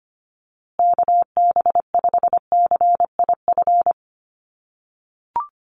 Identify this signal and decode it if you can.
Morse code, heard as a keyed tone, about 25 wpm, 700 Hz. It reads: KB5CIF
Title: USA Call Signs Repeated 3x 25wpm